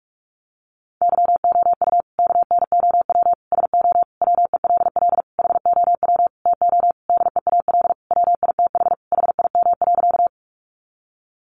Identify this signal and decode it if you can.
Morse code, heard as a keyed tone, about 44 wpm, 700 Hz. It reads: YOU KNOW SO WELL HOW TO DEAL WITH HIM.
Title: Sentences from Top 800 Words 45wpm